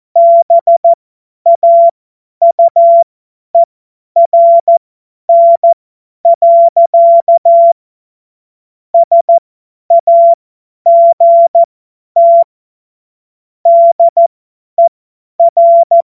Morse code, heard as a keyed tone, about 14 wpm, spaced as for 10 wpm, 680 Hz. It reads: BAUERN. SAGT DER
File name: Witz-09-September-2025-50-F70